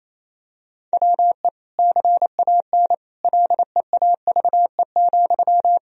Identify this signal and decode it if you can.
Morse code, heard as a keyed tone, about 28 wpm, 700 Hz. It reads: WE CAN LEAVE,
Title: Sentences from Top 1000 Words - Review Part 4 - 28wpm